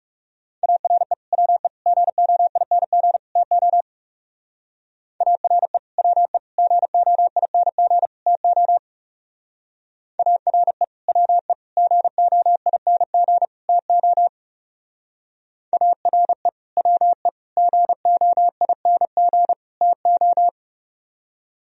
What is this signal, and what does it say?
Morse code, heard as a keyed tone, about 34 wpm, 700 Hz. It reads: ARE WE GOING TO ARE WE GOING TO ARE WE GOING TO ARE WE GOING TO
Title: Speed-Racing - Sets of 4 Words 30wpm